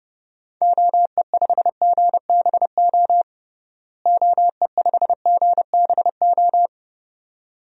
Morse code, heard as a keyed tone, about 32 wpm, 700 Hz. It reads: OE5GBO OE5GBO
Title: DXCC-100 Call Signs - 6 Characters - Repeated 2x - 30wpm